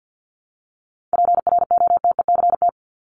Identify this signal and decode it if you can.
Morse code, heard as a keyed tone, about 50 wpm, 700 Hz. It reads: PROTECT
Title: Top 751-1000 Words 50wpm